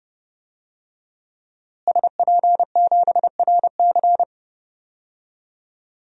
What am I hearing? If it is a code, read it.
Morse code, heard as a keyed tone, about 30 wpm, 700 Hz. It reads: SP7RC
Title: DXCC-100 Call Signs - 5 Characters - Repeated 2x - 30wpm